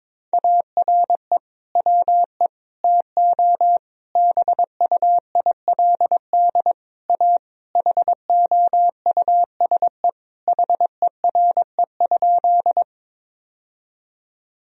Morse code, heard as a keyed tone, about 22 wpm, 700 Hz. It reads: ARE WE TO BUILD A HOUSE HERE?
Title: Sentences from Top 1000 Words - Review Part 1 - 22wpm